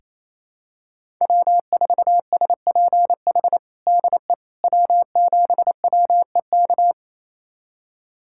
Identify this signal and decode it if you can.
Morse code, heard as a keyed tone, about 28 wpm, 700 Hz. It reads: W4SPH DE W7WEK
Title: USA Callsign DE Callsign - 5 Characters - 28wpm